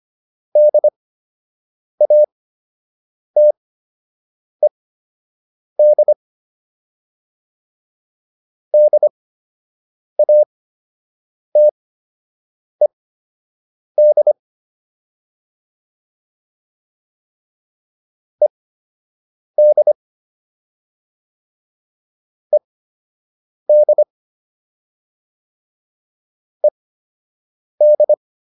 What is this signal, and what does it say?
Morse code, heard as a keyed tone, about 25 wpm, spaced as for 7 wpm, 600 Hz. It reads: DATED DATED ED ED ED